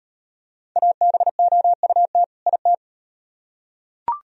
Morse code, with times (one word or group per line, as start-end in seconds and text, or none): 0.8-2.2 ABOUT
2.5-2.7 IT